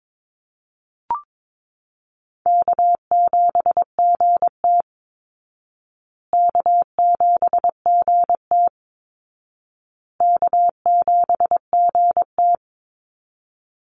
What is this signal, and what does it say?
Morse code, heard as a keyed tone, about 22 wpm, 700 Hz. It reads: K7GT K7GT K7GT
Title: Easy Call Signs Repeated 3x - 22wpm